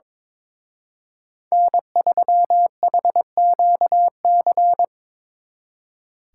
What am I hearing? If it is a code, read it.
Morse code, heard as a keyed tone, about 26 wpm, 700 Hz. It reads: N3HQC
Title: Speed Racing - USA 1x3 Call Signs 22wpm